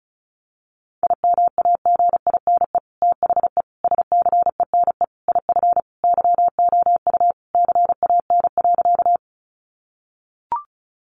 Morse code, heard as a keyed tone, about 34 wpm, 700 Hz. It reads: IMAGINE THE SCENE IF YOU CAN.
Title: Sentences from Top 1400 Words 35wpm